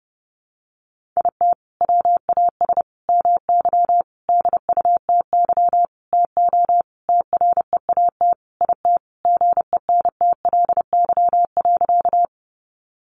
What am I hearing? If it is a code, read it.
Morse code, heard as a keyed tone, about 30 wpm, 700 Hz. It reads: IT WAS MY DUTY TO TREAT IT GENTLY.
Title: Sentences from Top 1500 Words 30wpm